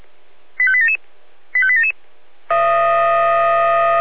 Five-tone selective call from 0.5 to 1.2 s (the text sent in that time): 85877